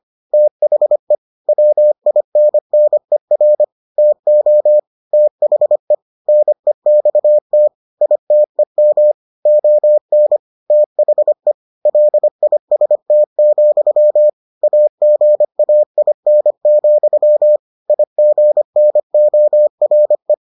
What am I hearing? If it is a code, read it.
Morse code, heard as a keyed tone, about 25 wpm, 600 Hz. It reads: THE WINNER TO THE NEXT ITEM ON THE LIST, AGAIN, IGNORE